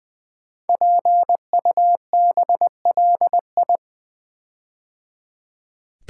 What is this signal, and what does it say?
Morse code, heard as a keyed tone, about 20 wpm, 700 Hz. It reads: PUBLI
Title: Build A Word - 5 Letters At A Time - 10-Letter Words - 20wpm